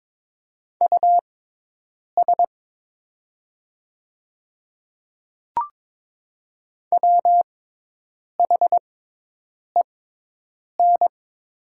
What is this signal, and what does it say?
Morse code, heard as a keyed tone, about 22 wpm, spaced as for 8 wpm, 700 Hz. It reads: US WHEN
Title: 6x Character Spacing - Top 200 Words - 22wpm